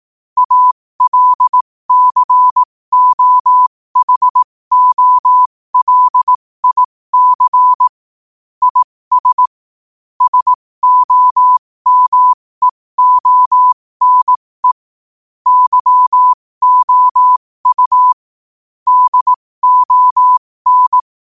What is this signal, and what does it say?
Morse code, heard as a keyed tone, about 18 wpm, spaced as for 15 wpm, 1000 Hz. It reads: ALCOHOLIC IS SOMEONE YOU DON
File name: qotdfiles / 15wpm